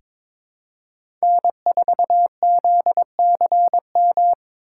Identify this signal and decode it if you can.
Morse code, heard as a keyed tone, about 22 wpm, 700 Hz. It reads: N4ZCM
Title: USA Call Signs - 5 Characters - 3x Character Spacing - 22wpm